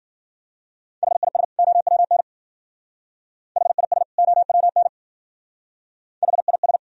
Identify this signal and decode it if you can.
Morse code, heard as a keyed tone, about 53 wpm, 700 Hz. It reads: HIS OWN HIS OWN HIS